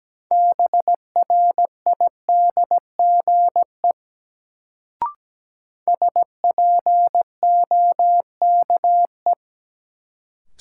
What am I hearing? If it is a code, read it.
Morse code, heard as a keyed tone, about 17 wpm, 700 Hz. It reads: BRIDGE SPOKE
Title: Sentences from Top 1200 Words - New Word Drill - 17wpm